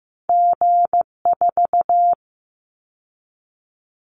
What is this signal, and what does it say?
Morse code, heard as a keyed tone, about 15 wpm, 700 Hz. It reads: G4
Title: Callsign Prefixes for Top 25 DXCC - 15wpm